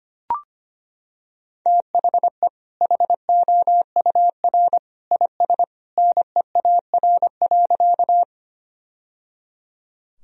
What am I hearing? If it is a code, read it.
Morse code, heard as a keyed tone, about 25 wpm, 700 Hz. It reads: THE HOUR IS NEAR.
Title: Sentences from Top 1000 Words - Review Part 1 - 25wpm